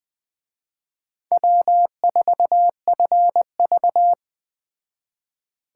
Morse code, heard as a keyed tone, about 20 wpm, 700 Hz. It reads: W4FV
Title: Speed Racing - USA 1x2 Call Signs 17wpm